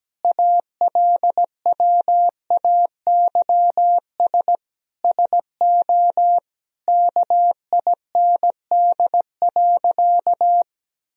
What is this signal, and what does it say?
Morse code, heard as a keyed tone, about 17 wpm, 700 Hz. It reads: ALWAYS SO KIND.
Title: Sentences from Top 1000 Words - Review Part 3 - 17wpm